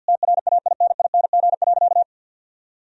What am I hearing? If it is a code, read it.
Morse code, heard as a keyed tone, about 50 wpm, 700 Hz. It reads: TRAINING.
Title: Sentences from Top 1000 Words - Review Part 4 - 50wpm